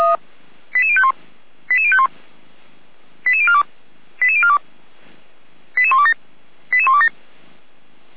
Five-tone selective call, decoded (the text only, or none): 88051 88051 88042 88042 88127 88127